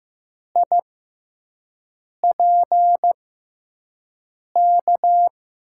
Morse code, heard as a keyed tone, about 15 wpm, spaced as for 5 wpm, 700 Hz. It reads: IPK
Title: USA Call Signs - 6 Characters - 6x Character Spacing - 15wpm